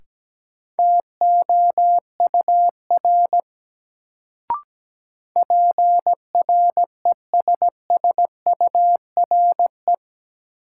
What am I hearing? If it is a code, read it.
Morse code, heard as a keyed tone, about 17 wpm, 700 Hz. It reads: TOUR PRESSURE
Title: Sentences from Top 1400 Words - New Word Drill Alt - 17wpm